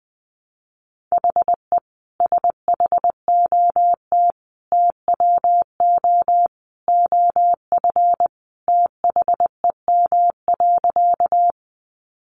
Morse code, heard as a keyed tone, about 20 wpm, 700 Hz. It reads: HE SHOT TWO OF THEM.
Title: Sentences from Top 1100 Words 20wpm